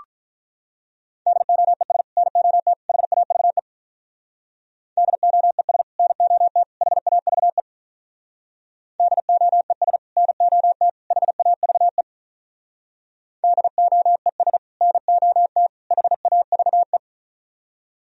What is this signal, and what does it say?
Morse code, heard as a keyed tone, about 46 wpm, 700 Hz. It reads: DOES NOT HAVE DOES NOT HAVE DOES NOT HAVE DOES NOT HAVE